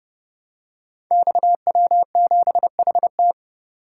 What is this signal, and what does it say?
Morse code, heard as a keyed tone, about 30 wpm, 700 Hz. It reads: XW7HT
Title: Code Groups - 5 Letter:Numbers - 30wpm